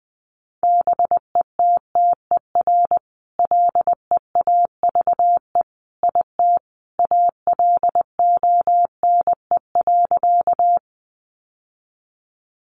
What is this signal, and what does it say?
Morse code, heard as a keyed tone, about 19 wpm, 700 Hz. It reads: BETTER LEAVE IT ALONE.
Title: Sentences from Top 1000 Words 20wpm